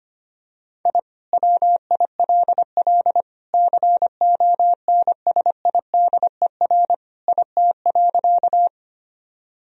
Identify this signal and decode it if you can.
Morse code, heard as a keyed tone, about 25 wpm, 700 Hz. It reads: I WILL CONSIDER IT.
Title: Sentences from Top 1000 Words - Review Part 4 - 25wpm